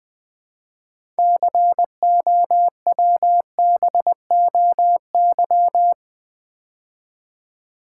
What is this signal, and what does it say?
Morse code, heard as a keyed tone, about 20 wpm, 700 Hz. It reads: COWBOY
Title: Build A Word - 2 Letters At A Time - 6-Letter Words - 20wpm